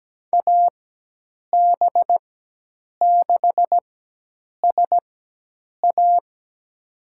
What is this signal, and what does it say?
Morse code, heard as a keyed tone, about 17 wpm, spaced as for 8 wpm, 700 Hz. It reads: AB6SA
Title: USA Call Signs - 5 Characters - 4x Character Spacing - 17wpm